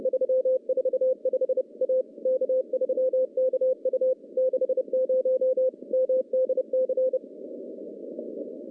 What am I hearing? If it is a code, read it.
Morse code, heard as a keyed tone, about 31 wpm, 520 Hz. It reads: 345 A K3KU 60 MDC
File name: contests / CW-SS-k5tr- -0800